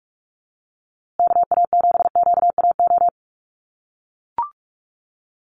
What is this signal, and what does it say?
Morse code, heard as a keyed tone, about 44 wpm, 700 Hz. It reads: KA7QAO
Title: USA Call Signs Repeated 2x 45wpm